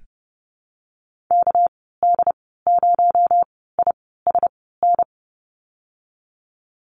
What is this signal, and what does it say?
Morse code, heard as a keyed tone, about 30 wpm, spaced as for 17 wpm, 700 Hz. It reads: KD0ISN
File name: Lesson 37g - Letter x - Call Signs 30wpm